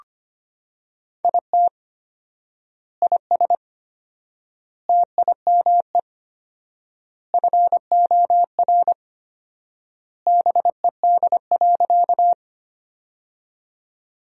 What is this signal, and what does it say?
Morse code, heard as a keyed tone, about 25 wpm, 700 Hz. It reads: IT IS TIME FOR BED.
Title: 4x Word-Spacing - Sentences from Top 500 Words - 25wpm